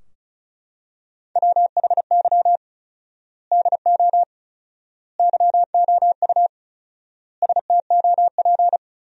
Morse code, heard as a keyed tone, about 36 wpm, 700 Hz. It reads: WHY DO YOU STOP
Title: 4x Word-Spacing - Sentences from Top 300 Words - 35wpm